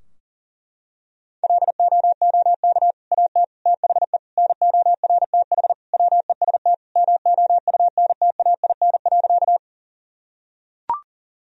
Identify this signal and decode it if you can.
Morse code, heard as a keyed tone, about 40 wpm, 700 Hz. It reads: LOOK AT THE NORTH WEST MOUNTAIN.
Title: Sentences from Top 1000 Words - Review Part 2 - 40wpm